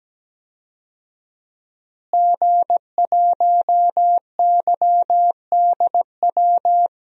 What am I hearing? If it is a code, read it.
Morse code, heard as a keyed tone, about 17 wpm, 700 Hz. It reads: G1YDW
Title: DXCC-100 Call Signs - 5 Characters - Repeated 2x - 17wpm